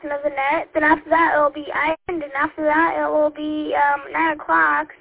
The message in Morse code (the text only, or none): none